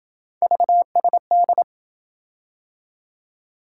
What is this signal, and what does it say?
Morse code, heard as a keyed tone, about 27 wpm, 700 Hz. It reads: VSD